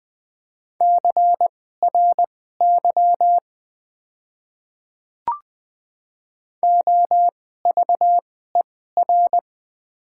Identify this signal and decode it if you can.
Morse code, heard as a keyed tone, about 20 wpm, spaced as for 15 wpm, 700 Hz. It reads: CRY OVER
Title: 2x Character Spacing - Top 500 Words - 20wpm